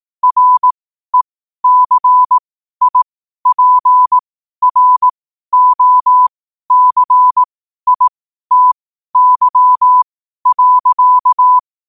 Morse code, heard as a keyed tone, about 18 wpm, spaced as for 12 wpm, 1000 Hz. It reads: RECIPROCITY.
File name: qotdfiles / 12wpm